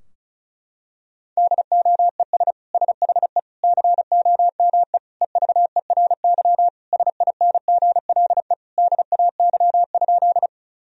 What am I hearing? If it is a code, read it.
Morse code, heard as a keyed tone, about 34 wpm, 700 Hz. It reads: DOES SHE COME EVERY SINGLE DAY?
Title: Sentences from Top 1000 Words - Review Part 4 - 35wpm